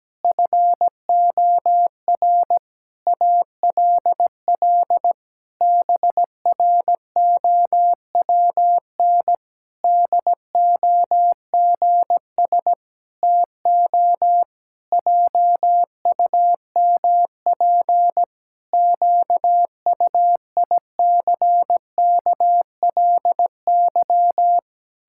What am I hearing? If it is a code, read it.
Morse code, heard as a keyed tone, about 17 wpm, 700 Hz. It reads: FOR ALL BROWN DOGS TO JUMP QUICKLY
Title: Pangram Challenge v2 - 17wpm